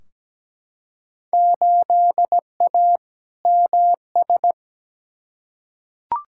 Morse code, heard as a keyed tone, about 17 wpm, 700 Hz. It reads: 8A MS
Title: Field Day Exchanges - 17wpm